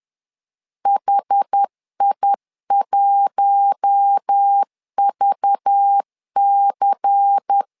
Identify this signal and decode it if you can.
Morse code, heard as a keyed tone, about 10 wpm, 800 Hz. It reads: HI1VC
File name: LEZIONI / 5 LBVC / 5 LBVC 10wpm 2349